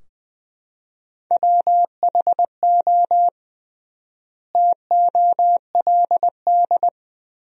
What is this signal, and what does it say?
Morse code, heard as a keyed tone, about 20 wpm, 700 Hz. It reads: WHO TOLD
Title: 3x Word-Spacing - Sentences from Top 400 Words - 20wpm